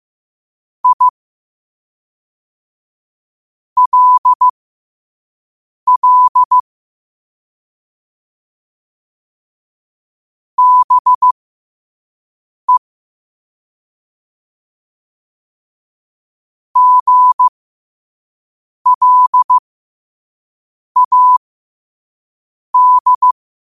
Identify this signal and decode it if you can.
Morse code, heard as a keyed tone, about 15 wpm, spaced as for 5 wpm, 1000 Hz. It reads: ILL BE GLAD
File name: qotdfiles / 5wpm